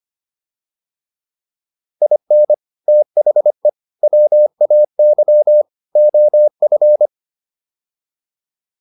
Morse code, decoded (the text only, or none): IN THE WAY OF